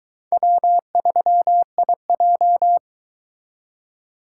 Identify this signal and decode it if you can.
Morse code, heard as a keyed tone, about 23 wpm, 700 Hz. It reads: W3IJ